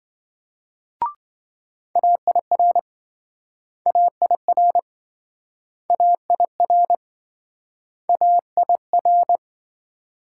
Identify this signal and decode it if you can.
Morse code, heard as a keyed tone, about 28 wpm, 700 Hz. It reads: AIR AIR AIR AIR